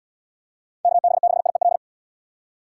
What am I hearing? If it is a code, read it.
Morse code, heard as a keyed tone, about 73 wpm, 700 Hz. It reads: KB6IEL